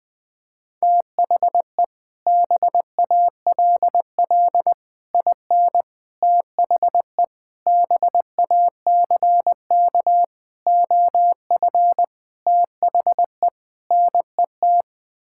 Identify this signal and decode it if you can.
Morse code, heard as a keyed tone, about 20 wpm, 700 Hz. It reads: THE BALL IN THE BACK OF THE NET
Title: Sets of 8 Words - 20wpm